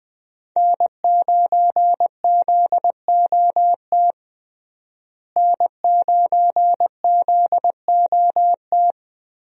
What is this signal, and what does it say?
Morse code, heard as a keyed tone, about 21 wpm, 700 Hz. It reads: N9ZOT N9ZOT
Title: USA Call Signs - 5 Characters - Repeated 2x - 20wpm